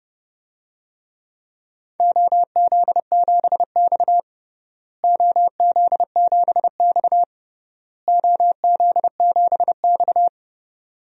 Morse code, heard as a keyed tone, about 30 wpm, 700 Hz. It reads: OZ7X OZ7X OZ7X
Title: DXCC-100 Call Signs - 4 Characters - Repeated 3x - 30wpm